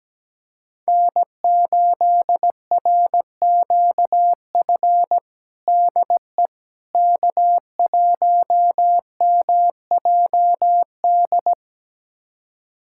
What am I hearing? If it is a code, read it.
Morse code, heard as a keyed tone, about 17 wpm, 700 Hz. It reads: N8RQF DE K1MJD
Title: USA Callsign DE Callsign - 5 Characters - 17wpm